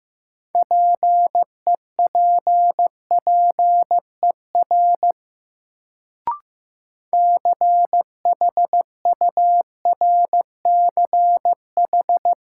Morse code, heard as a keyed tone, about 15 wpm, 700 Hz. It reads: PEPPER CHURCH
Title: Sets of 5 Words - New Word Drill - 15wpm